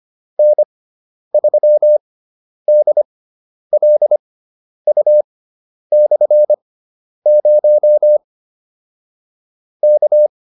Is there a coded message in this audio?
Yes: Morse code from 0.4 to 10.3 s, heard as a keyed tone, about 25 wpm, spaced as for 10 wpm, 600 Hz. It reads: N3DLU/0 K